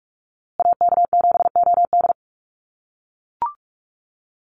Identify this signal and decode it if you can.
Morse code, heard as a keyed tone, about 44 wpm, 700 Hz. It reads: AK7OD
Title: USA Call Signs Repeated 3x 45wpm